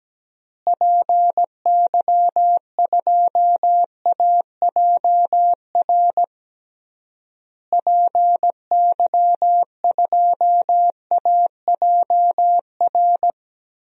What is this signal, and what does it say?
Morse code, heard as a keyed tone, about 17 wpm, 700 Hz. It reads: PY2AJR PY2AJR
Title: DXCC-100 Call Signs - 6 Characters - Repeated 3x - 17wpm